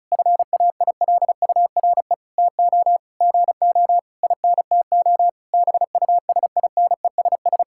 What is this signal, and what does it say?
Morse code, heard as a keyed tone, about 34 wpm, 700 Hz. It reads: FAILURE TO GO INTO BUSINESS
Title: Sentences from Top 1000 Words - Review Part 6 - 35wpm